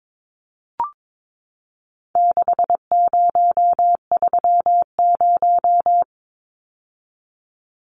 Morse code, heard as a keyed tone, about 22 wpm, 700 Hz. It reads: 6030
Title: Numbers - 4-Digit 22wpm